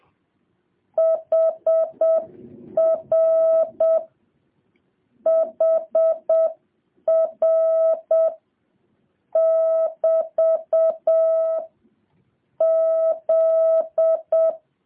Morse code, heard as a keyed tone, about 7 wpm, 650 Hz. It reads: HR HR <BT> Z